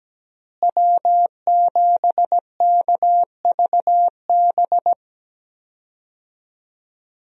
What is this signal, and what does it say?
Morse code, heard as a keyed tone, about 17 wpm, 700 Hz. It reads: W7KVB
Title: USA Call Signs - 5 Characters - Repeated 3x - 17wpm